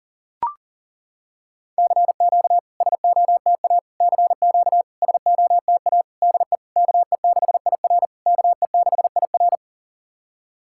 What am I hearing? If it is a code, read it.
Morse code, heard as a keyed tone, about 40 wpm, 690 Hz. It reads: CQ SOTA CQ SOTA DE KE6IR KE6IR
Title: SOTA QSO Practice 40wpm